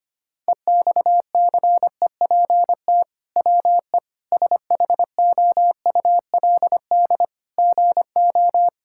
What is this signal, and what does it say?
Morse code, heard as a keyed tone, about 25 wpm, 700 Hz. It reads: EXCEPT WE SHOULD GO
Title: Sentences from Top 1600 Words 25wpm